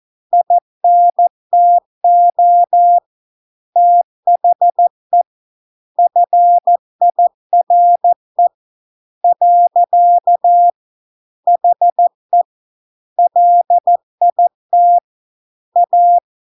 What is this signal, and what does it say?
Morse code, heard as a keyed tone, about 14 wpm, 700 Hz. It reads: INTO THE FIRE . HE LIT A